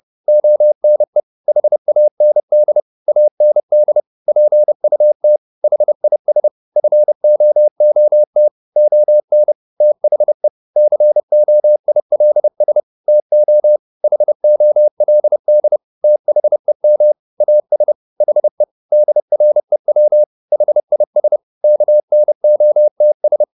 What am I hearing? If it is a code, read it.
Morse code, heard as a keyed tone, about 30 wpm, 600 Hz. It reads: ONE HAND AND PUT HIS FOOT ON THE COILS TO HOLD THEM AS HE DREW HIS KNOTS
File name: the old man and the sea / the old man and the sea 1 30wpm